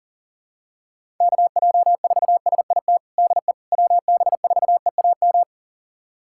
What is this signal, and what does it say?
Morse code, heard as a keyed tone, about 37 wpm, 700 Hz. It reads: KJ4SIT DE WB4EAM
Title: USA Callsign DE Callsign - 6 Characters - 40wpm